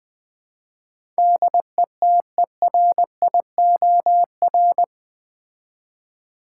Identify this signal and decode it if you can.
Morse code, heard as a keyed tone, about 20 wpm, 700 Hz. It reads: DETERIOR
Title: Build A Word - 4 Letters At A Time - 11-Letter Words - 20wpm